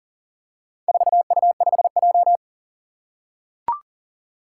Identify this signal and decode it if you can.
Morse code, heard as a keyed tone, about 40 wpm, 690 Hz. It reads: 4U5J
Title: Four Challenging Letter-Number Combinations 40wpm